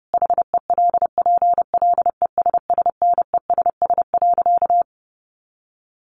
Morse code, heard as a keyed tone, about 30 wpm, 700 Hz. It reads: HELPLESSNESS.
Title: A Sound Of Thunder 30wpm